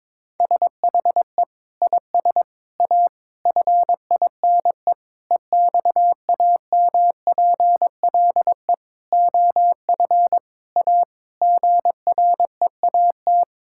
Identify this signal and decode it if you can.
Morse code, heard as a keyed tone, about 22 wpm, 700 Hz. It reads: SHE IS A FINE EXAMPLE OF A GREAT
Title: Sentences from Top 1000 Words - Review Part 6 - 22wpm